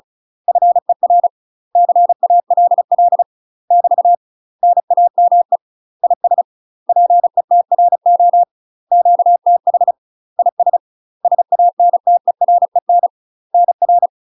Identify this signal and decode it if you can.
Morse code, heard as a keyed tone, about 35 wpm, 700 Hz. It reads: FER CALL <BT> NAME IS PETRO QTH IS SANTEREN NR
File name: UK MASTER QSOs TRAINING / 35WPM / UK MASTER QSOs 0043